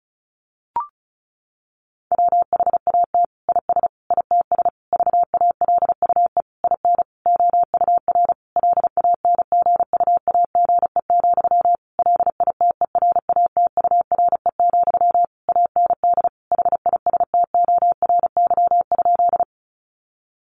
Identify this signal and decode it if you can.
Morse code, heard as a keyed tone, about 34 wpm, 700 Hz. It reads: WHAT IS ITS VALUE IN OUR LANGUAGE, LITERATURE, AND HISTORY?
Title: Sentences from Top 600 Words 35wpm